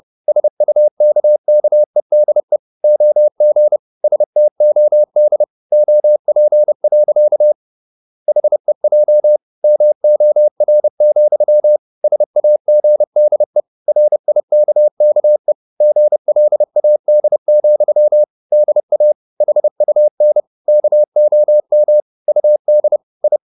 Morse code, heard as a keyed tone, about 30 wpm, 600 Hz. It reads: SUKKEDE OG STOD OP. HEJ MOR, SAGDE RIKKE GLAD, DA HUN KOM UD I
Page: Ferie 30 wpm